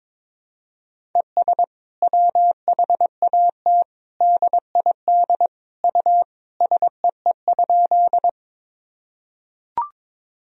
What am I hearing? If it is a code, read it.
Morse code, heard as a keyed tone, about 22 wpm, 700 Hz. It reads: ES WHAT DID U SEE?
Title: Sentences from Top 100 Words and Abbreviations 22wpm